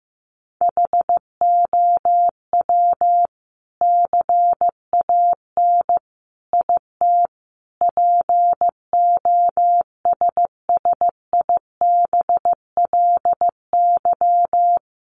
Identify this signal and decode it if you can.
Morse code, heard as a keyed tone, about 15 wpm, 700 Hz. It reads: HOW CAN IT POSSIBLY
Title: Sentences from Top 1300 Words 15wpm